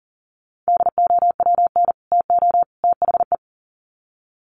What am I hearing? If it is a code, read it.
Morse code, heard as a keyed tone, about 40 wpm, 700 Hz. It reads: DOWN TO THE
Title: Sets of 3 Words 40wpm